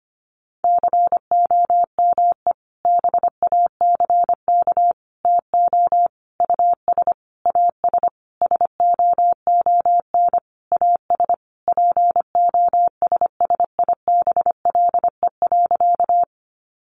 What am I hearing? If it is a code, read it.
Morse code, heard as a keyed tone, about 25 wpm, 700 Hz. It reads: COME BACK TO US AS SOON AS POSSIBLE.
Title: Sentences from Top 500 Words 25wpm